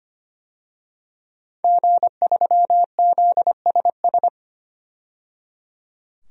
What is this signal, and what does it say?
Morse code, heard as a keyed tone, about 25 wpm, 700 Hz. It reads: G3ZSS
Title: DXCC-100 Call Signs - 5 Characters - 25wpm